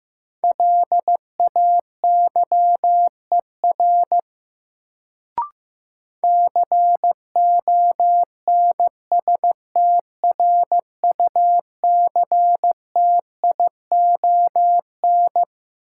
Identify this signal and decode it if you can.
Morse code, heard as a keyed tone, about 15 wpm, 700 Hz. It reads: LAYER CONSTRUCTION
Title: Sentences from Top 1600 Words - New Word Drill - 15wpm